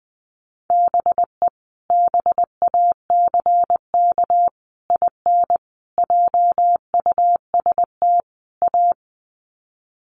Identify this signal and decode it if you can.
Morse code, heard as a keyed tone, about 20 wpm, 700 Hz. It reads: BE BACK IN JUST A
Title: Sets of 5 Words Encore 20wpm